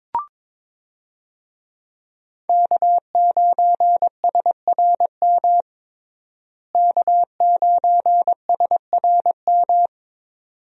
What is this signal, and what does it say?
Morse code, heard as a keyed tone, about 22 wpm, 700 Hz. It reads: K9SRM K9SRM
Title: USA Call Signs - 5 Characters - Repeated 2x - 22wpm